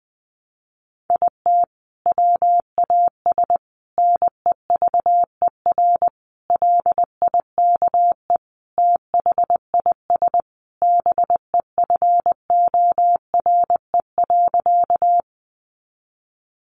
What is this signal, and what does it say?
Morse code, heard as a keyed tone, about 20 wpm, 700 Hz. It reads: IT WAS NEVER LIKE THIS BEFORE.
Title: A Sound Of Thunder 20wpm